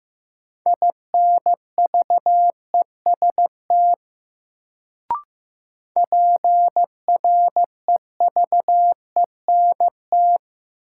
Morse code, heard as a keyed tone, about 15 wpm, 700 Hz. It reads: INVEST PREVENT
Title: Sentences from Top 1100 Words - New Word Drill - 15wpm